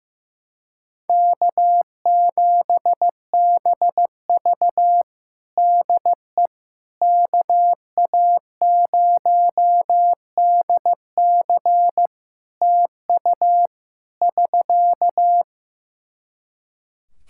Morse code, heard as a keyed tone, about 15 wpm, 700 Hz. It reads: K7BV DE KA0DC TU <SK>
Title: Practice QSO - Exchange 4 - Ending - 15wpm